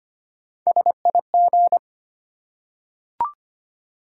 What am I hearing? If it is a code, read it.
Morse code, heard as a keyed tone, about 25 wpm, 700 Hz. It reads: SIG
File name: Top 90 QSO Elements - 4x Character Spacing - 25wpm